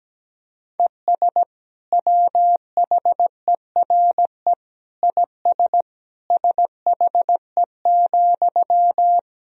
Morse code, heard as a keyed tone, about 17 wpm, 700 Hz. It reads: ES WHERE IS SHE,
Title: Sentences from Top 200 Words and Abbreviations 17wpm